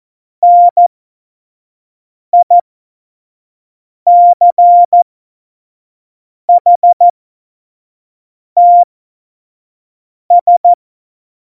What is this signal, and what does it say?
Morse code, heard as a keyed tone, about 14 wpm, spaced as for 5 wpm, 700 Hz. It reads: NICHTS
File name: Morsekurs / Witze / Witz 07. Januar 2020, 25 F70